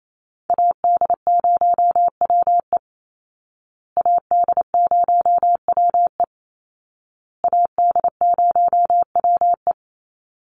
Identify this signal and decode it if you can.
Morse code, heard as a keyed tone, about 28 wpm, 700 Hz. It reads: AD0WE AD0WE AD0WE
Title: Call Signs Repeated 2x - 28wpm